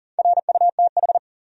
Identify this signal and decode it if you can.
Morse code, heard as a keyed tone, about 40 wpm, 700 Hz. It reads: RUTH
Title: USA Common Names - Max 4 Letters - 40wpm